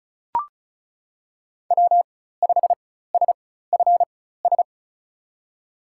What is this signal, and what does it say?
Morse code, heard as a keyed tone, about 34 wpm, spaced as for 16 wpm, 700 Hz. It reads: W5SFS
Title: USA Call Signs - 5 Characters - 4x Character Spacing - 35wpm